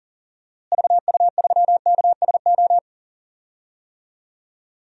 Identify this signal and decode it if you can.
Morse code, heard as a keyed tone, about 40 wpm, 700 Hz. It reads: VU3KSO
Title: DXCC-100 Call Signs - 6 Characters - Repeated 2x - 40wpm